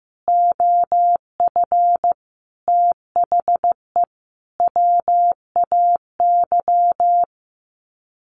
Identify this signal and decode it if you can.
Morse code, heard as a keyed tone, about 15 wpm, 700 Hz. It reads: OF THE WAY
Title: Sets of 5 Words Encore 15wpm